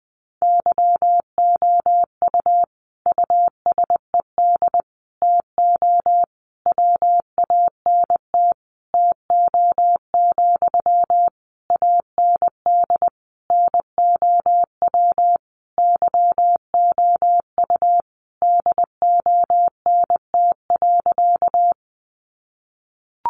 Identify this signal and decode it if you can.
Morse code, heard as a keyed tone, about 19 wpm, 700 Hz. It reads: YOU USED TO WANT TO, AND NOW YOU DONT.
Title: Sentences from Top 600 Words 20wpm